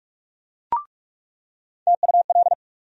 Morse code, heard as a keyed tone, about 42 wpm, 700 Hz. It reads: TUL
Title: USA Call Sign Postfixes - 3 Letters 45wpm